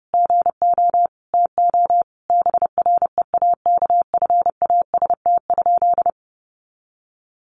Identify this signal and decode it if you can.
Morse code, heard as a keyed tone, about 30 wpm, 700 Hz. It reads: GO TO BREAKFAST?
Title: Sentences from Top 1500 Words 30wpm